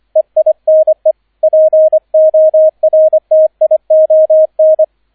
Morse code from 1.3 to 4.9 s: PORTION